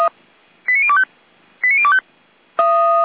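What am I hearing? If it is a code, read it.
Five-tone selective call (ZVEI), heard as quick runs of five tones, tones of about 70 ms. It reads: 89025 89025
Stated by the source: ZVEI five-tone selective call received on an SDR.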